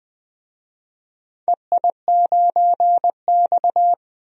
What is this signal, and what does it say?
Morse code, heard as a keyed tone, about 20 wpm, 700 Hz. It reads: EI9X
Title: DXCC-100 Call Signs - 4 Characters - 20wpm